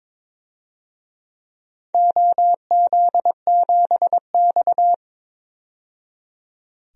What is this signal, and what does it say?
Morse code, heard as a keyed tone, about 22 wpm, 700 Hz. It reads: OZ7X
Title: DXCC-100 Call Signs - 4 Characters - 22wpm